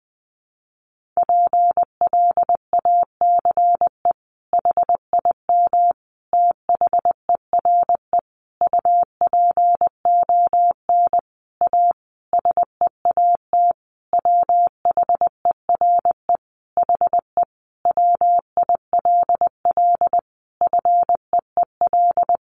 Morse code, heard as a keyed tone, about 20 wpm, 700 Hz. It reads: PLACE HIM THERE UPON A SEAT WHERE HE WILL FEEL
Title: Sentences from Top 1500 Words 20wpm